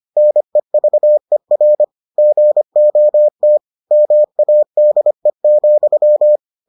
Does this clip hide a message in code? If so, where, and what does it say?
Yes: Morse code from 0.2 to 6.4 s, heard as a keyed tone, about 25 wpm, 600 Hz. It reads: NEVER GOT MADE,